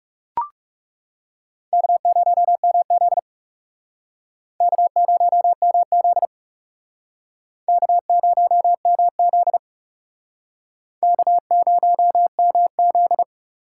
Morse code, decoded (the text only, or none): K0MZ K0MZ K0MZ K0MZ